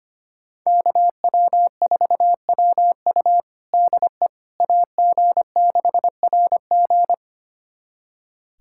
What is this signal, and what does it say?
Morse code, heard as a keyed tone, about 25 wpm, 700 Hz. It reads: KW4WU DE AG6RG
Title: USA Callsign DE Callsign - 5 Characters - 25wpm